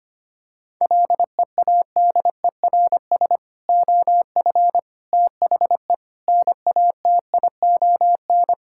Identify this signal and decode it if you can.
Morse code, heard as a keyed tone, about 25 wpm, 700 Hz. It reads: LEADERS OF THE NATION
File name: practice / 25wpm / Sentences with 3-Word Phrases - 25wpm